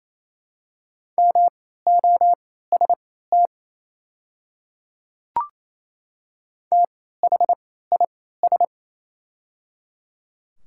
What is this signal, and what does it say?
Morse code, heard as a keyed tone, about 27 wpm, spaced as for 16 wpm, 700 Hz. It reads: MOST THIS
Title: 3x Character Spacing - Top 100 Words - 28wpm